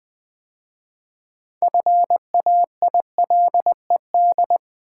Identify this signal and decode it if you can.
Morse code, heard as a keyed tone, about 20 wpm, 700 Hz. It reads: FAILED
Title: 6 Letter Words - First 3 Letters Given - 20wpm